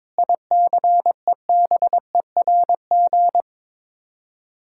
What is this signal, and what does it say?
Morse code, heard as a keyed tone, about 22 wpm, 700 Hz. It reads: ICEBERG
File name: practice / 22wpm / Sets of 8 Words - 22wpm